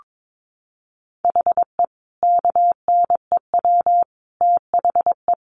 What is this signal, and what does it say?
Morse code, heard as a keyed tone, about 22 wpm, 700 Hz. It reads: HE KNEW THE
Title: Sentences from Top 1400 Words 22wpm